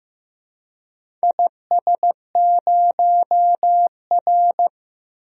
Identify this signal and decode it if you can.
Morse code, heard as a keyed tone, about 15 wpm, 700 Hz. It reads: IS0R
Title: DXCC-100 Call Signs - 4 Characters - 15wpm